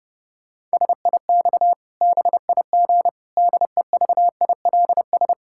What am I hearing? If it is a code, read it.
Morse code, heard as a keyed tone, about 30 wpm, 700 Hz. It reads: SIX BIG DEVILS
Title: Pangram Challenge v2 - 30wpm